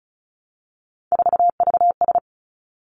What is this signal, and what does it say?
Morse code, heard as a keyed tone, about 36 wpm, 700 Hz. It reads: 4VS
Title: Three Challenging Letter-Number Combinations 35wpm